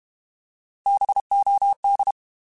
Morse code, heard as a keyed tone, about 32 wpm, 800 Hz. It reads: BOD
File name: Alphabetical / Alphabetical mp3s / Bs @30